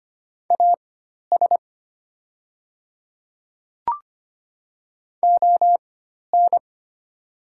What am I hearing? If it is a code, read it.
Morse code, heard as a keyed tone, about 25 wpm, spaced as for 12 wpm, 700 Hz. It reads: AS ON
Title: 4x Character Spacing - Top 100 Words - 25wpm